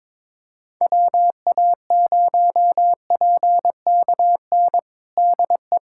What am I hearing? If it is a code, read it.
Morse code, heard as a keyed tone, about 22 wpm, 700 Hz. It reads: WA0PKN DE
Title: USA Callsign DE Callsign - 6 Characters - 22wpm